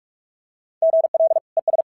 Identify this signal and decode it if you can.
Morse code, heard as a keyed tone, about 46 wpm, 650 Hz. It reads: GL ES
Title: SOTA QSO Practice 45wpm